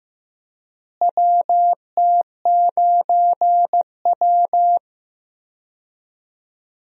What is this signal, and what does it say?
Morse code, heard as a keyed tone, about 15 wpm, 700 Hz. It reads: WT9W
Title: USA Call Signs - 4 Characters - Repeated 3x - 15wpm